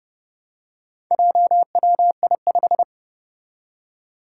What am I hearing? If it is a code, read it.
Morse code, heard as a keyed tone, about 30 wpm, 700 Hz. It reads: JWI5
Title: Four Challenging Letter-Number Combinations 30wpm